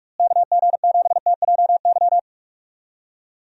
Morse code, heard as a keyed tone, about 46 wpm, 700 Hz. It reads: KG7TJY
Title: USA Call Signs - 6 Characters - Repeated 3x - 45wpm